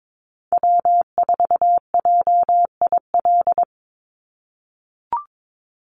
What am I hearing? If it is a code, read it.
Morse code, heard as a keyed tone, about 22 wpm, 700 Hz. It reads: W4JIL
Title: USA Call Signs 22wpm